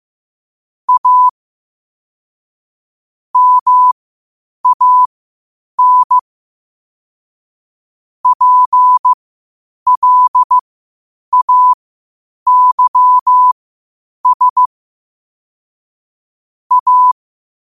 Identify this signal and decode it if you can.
Morse code, heard as a keyed tone, about 15 wpm, spaced as for 8 wpm, 1000 Hz. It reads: A MAN PLAYS A
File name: qotdfiles / 8wpm